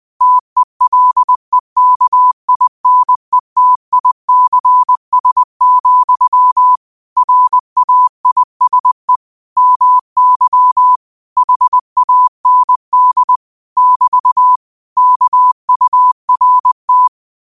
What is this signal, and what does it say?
Morse code, heard as a keyed tone, about 20 wpm, 1000 Hz. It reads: TELEKINETICS, RAISE MY HAND <BT> KURT
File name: qotdfiles / 20wpm